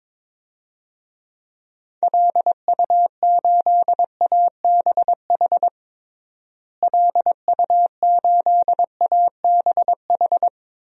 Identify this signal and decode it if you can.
Morse code, heard as a keyed tone, about 22 wpm, 700 Hz. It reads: LU8ABH LU8ABH
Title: DXCC-100 Call Signs - 6 Characters - Repeated 2x - 22wpm